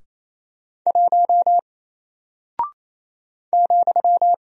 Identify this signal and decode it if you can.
Morse code, heard as a keyed tone, about 28 wpm, 700 Hz. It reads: 1 ,
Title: Single Character (Letter:Number:Punctuation:Prosign) Alt 28wpm